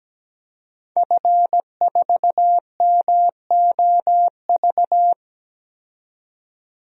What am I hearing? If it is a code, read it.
Morse code, heard as a keyed tone, about 17 wpm, 700 Hz. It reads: F4MOV
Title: DXCC-100 Call Signs - 5 Characters - 17wpm